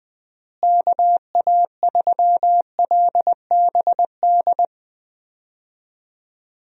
Morse code, heard as a keyed tone, about 20 wpm, 700 Hz. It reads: KA3LBD
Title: USA Call Signs - 6 Characters - Repeated 2x - 20wpm